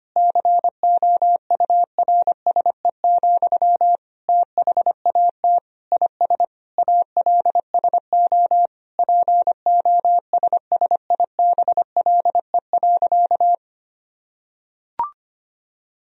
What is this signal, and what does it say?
Morse code, heard as a keyed tone, about 25 wpm, 700 Hz. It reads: COURSE, THAT IS ALSO POSSIBLE.
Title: Sentences from Top 1000 Words - Review Part 1 - 25wpm